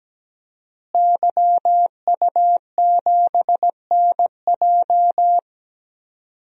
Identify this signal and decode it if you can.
Morse code, heard as a keyed tone, about 17 wpm, 700 Hz. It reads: YU7NJ
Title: DXCC-100 Call Signs - 5 Characters - Repeated 2x - 17wpm